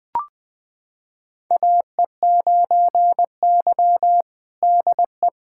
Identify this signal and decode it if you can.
Morse code, heard as a keyed tone, about 20 wpm, 690 Hz. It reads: AE9Y DE
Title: USA Callsign DE Callsign 20wpm